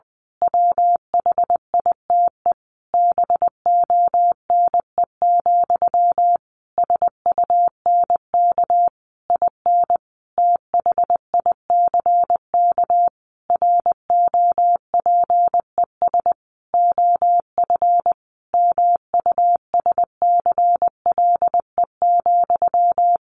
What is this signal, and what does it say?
Morse code, heard as a keyed tone, about 20 wpm, 700 Hz. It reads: WHITE BONE, SUNK IN THICK ROPES OF MUSCLE,
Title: A Sound Of Thunder 20wpm